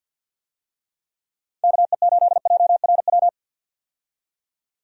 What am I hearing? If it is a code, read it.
Morse code, heard as a keyed tone, about 48 wpm, 700 Hz. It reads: KE8JRW
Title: USA Call Signs - 6 Characters - 50wpm